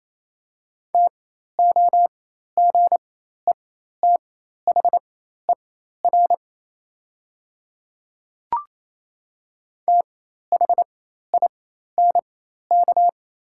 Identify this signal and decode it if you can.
Morse code, heard as a keyed tone, about 28 wpm, spaced as for 13 wpm, 700 Hz. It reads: TOGETHER THINK
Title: 4x Character Spacing - Top 300 Words - 28wpm